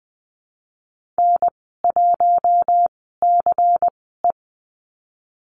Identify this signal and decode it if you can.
Morse code, heard as a keyed tone, about 20 wpm, spaced as for 15 wpm, 700 Hz. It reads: N1CE
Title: 179 Lesson 29g - Slash - Call Signs 20wpm